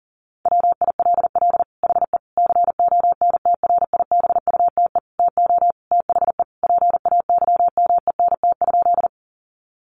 Morse code, heard as a keyed tone, about 40 wpm, 700 Hz. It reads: WILL HE CONTRIBUTE TO THE PAYMENT?
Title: Sentences from Top 1400 Words 40wpm